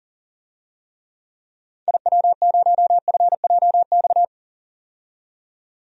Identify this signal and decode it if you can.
Morse code, heard as a keyed tone, about 40 wpm, 700 Hz. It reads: IW0FJX
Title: DXCC-100 Call Signs - 6 Characters - 40wpm